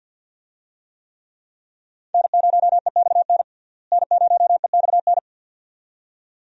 Morse code, heard as a keyed tone, about 50 wpm, 700 Hz. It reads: N0EXN N0EXN
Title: USA Call Signs - 5 Characters - Repeated 2x - 50wpm